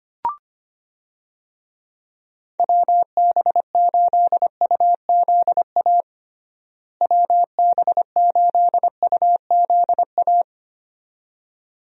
Morse code, heard as a keyed tone, about 25 wpm, 700 Hz. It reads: WB8UZA WB8UZA
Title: USA Call Signs - 6 Characters - Repeated 2x - 25wpm